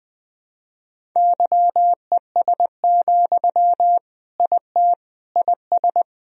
Morse code, heard as a keyed tone, about 21 wpm, 700 Hz. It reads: YES, IT IS
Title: Sentences from Top 1000 Words - Review Part 1 - 20wpm